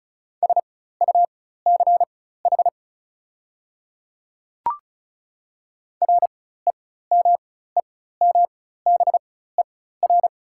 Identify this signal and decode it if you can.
Morse code, heard as a keyed tone, about 36 wpm, spaced as for 16 wpm, 690 Hz. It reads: SUCH REMEMBER
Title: 4x Character Spacing - Top 500 Words - 35wpm